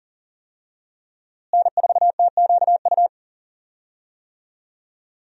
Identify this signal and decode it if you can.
Morse code, heard as a keyed tone, about 39 wpm, 700 Hz. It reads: N4TQU
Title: USA Call Signs - 5 Characters - Repeated 2x - 40wpm